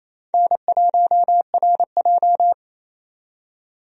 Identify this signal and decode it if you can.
Morse code, heard as a keyed tone, about 28 wpm, 700 Hz. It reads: N1RJ
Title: USA Call Signs - 4 Characters - 28wpm